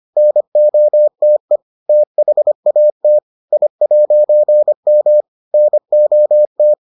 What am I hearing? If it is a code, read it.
Morse code, heard as a keyed tone, about 25 wpm, 600 Hz. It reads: NOTE THAT I'M NOT